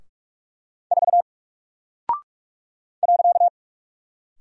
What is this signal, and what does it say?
Morse code, heard as a keyed tone, about 44 wpm, 700 Hz. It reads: 4 .
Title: Single Character (Letter:Number:Punctuation:Prosign) Alt 45wpm